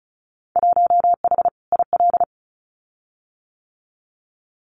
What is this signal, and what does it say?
Morse code, heard as a keyed tone, about 34 wpm, 700 Hz. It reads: 1H IL
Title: Winter Field Day Exchanges 35wpm